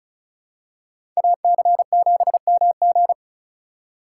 Morse code, heard as a keyed tone, about 36 wpm, 700 Hz. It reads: AC7MG
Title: USA Call Signs - 5 Characters - 2x Character Spacing - 35wpm